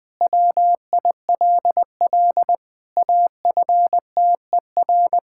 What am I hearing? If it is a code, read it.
Morse code, heard as a keyed tone, about 21 wpm, 700 Hz. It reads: WILL AFTER
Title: Sentences from Top 1600 Words 20wpm